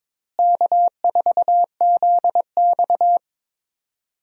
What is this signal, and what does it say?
Morse code, heard as a keyed tone, about 22 wpm, 700 Hz. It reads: K4ZX
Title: USA Call Signs - 4 Characters - Repeated 3x - 22wpm